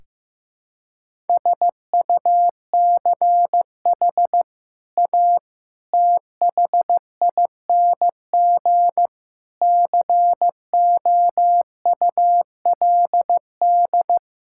Morse code, heard as a keyed tone, about 16 wpm, 700 Hz. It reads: SUCH A THING COULD
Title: Sentences from Top 1000 Words - Review Part 6 - 15wpm